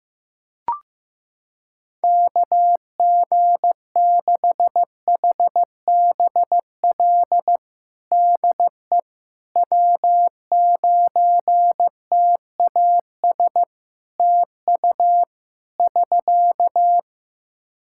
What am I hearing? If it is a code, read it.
Morse code, heard as a keyed tone, about 15 wpm, 700 Hz. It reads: KG6HBL DE W9TAS TU <SK>
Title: Practice QSO - Exchange 4 - Ending - 15wpm